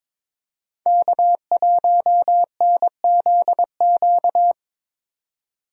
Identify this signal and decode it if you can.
Morse code, heard as a keyed tone, about 21 wpm, 700 Hz. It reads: K1NZQ
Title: USA Call Signs - 5 Characters - 22wpm